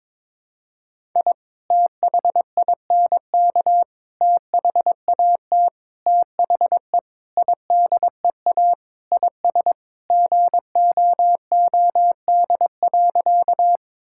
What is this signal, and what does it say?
Morse code, heard as a keyed tone, about 22 wpm, 700 Hz. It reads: I THINK THAT THE IDEA IS GOOD.
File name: practice / 22wpm / Sentences with 4-Word Phrases - 22wpm